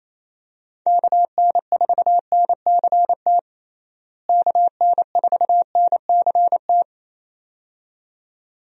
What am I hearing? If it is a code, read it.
Morse code, heard as a keyed tone, about 28 wpm, 700 Hz. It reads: KN4NCT KN4NCT
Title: USA Call Signs - 6 Characters - Repeated 2x - 28wpm